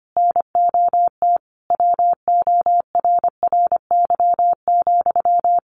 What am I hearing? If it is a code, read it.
Morse code, heard as a keyed tone, about 25 wpm, 700 Hz. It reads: NOT WORRY,
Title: Sentences from Top 1300 Words 25wpm